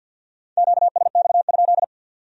Morse code, heard as a keyed tone, about 65 wpm, 700 Hz. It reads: QSX?